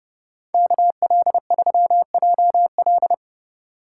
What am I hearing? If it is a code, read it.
Morse code, heard as a keyed tone, about 28 wpm, 700 Hz. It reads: KL3JL